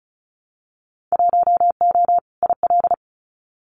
Morse code, heard as a keyed tone, about 34 wpm, 700 Hz. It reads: 1O IL
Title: Winter Field Day Exchanges 35wpm